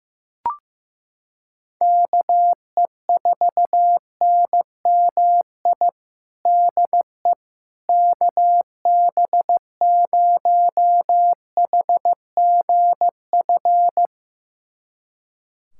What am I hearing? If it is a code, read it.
Morse code, heard as a keyed tone, about 15 wpm, 700 Hz. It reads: KE4NMI DE KB0HGF
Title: USA Callsign DE Callsign - 6 Characters - 15wpm